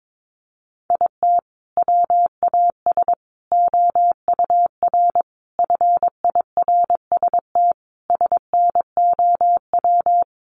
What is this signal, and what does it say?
Morse code, heard as a keyed tone, about 22 wpm, 700 Hz. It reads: IT WAS OUR FIRST SNOW
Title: Sentences from Top 1500 Words 22wpm